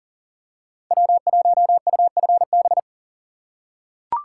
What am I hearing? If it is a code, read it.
Morse code, heard as a keyed tone, about 40 wpm, 700 Hz. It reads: W1UFB
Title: USA Call Signs - 4x Character Spacing - 40wpm